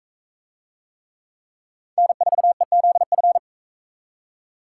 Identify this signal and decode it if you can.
Morse code, heard as a keyed tone, about 41 wpm, 700 Hz. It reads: N4EZF